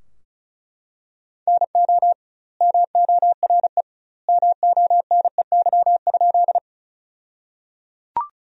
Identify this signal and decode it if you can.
Morse code, heard as a keyed tone, about 36 wpm, 700 Hz. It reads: NO MORE MONEY?
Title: 2x Word-Spacing - Sentences from Top 400 Words - 35wpm